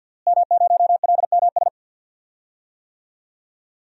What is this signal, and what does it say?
Morse code, heard as a keyed tone, about 52 wpm, 700 Hz. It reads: M0LMS
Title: DXCC-100 Call Signs - 5 Characters - Repeated 2x - 50wpm